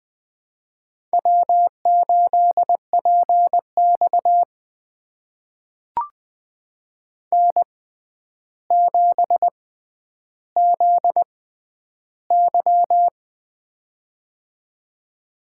Morse code, heard as a keyed tone, about 20 wpm, spaced as for 10 wpm, 700 Hz. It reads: W8PX N7ZY
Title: USA Call Signs - 4 Characters - 6x Character Spacing - 20wpm